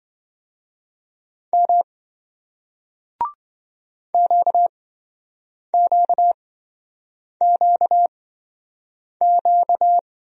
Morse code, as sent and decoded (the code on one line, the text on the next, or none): -- / --.- --.- --.- --.-
M QQQQ